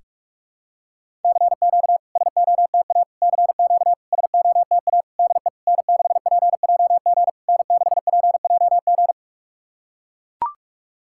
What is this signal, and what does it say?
Morse code, heard as a keyed tone, about 41 wpm, 690 Hz. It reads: CQ SOTA CQ SOTA DE N6PJG N6PJG
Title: SOTA QSO Practice 45wpm